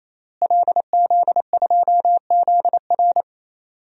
Morse code, heard as a keyed tone, about 28 wpm, 700 Hz. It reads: LZ2ZR
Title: DXCC-100 Call Signs - 5 Characters - Repeated 3x - 28wpm